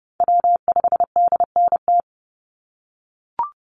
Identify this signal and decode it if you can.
Morse code, heard as a keyed tone, about 30 wpm, 700 Hz. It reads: W5DNT
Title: Call Signs - 30wpm